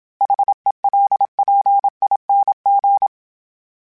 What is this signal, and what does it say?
Morse code, heard as a keyed tone, about 27 wpm, 800 Hz. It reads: HELPING
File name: Special INGs-mp3s / Special INGs-mp3s / Sngl Ing @25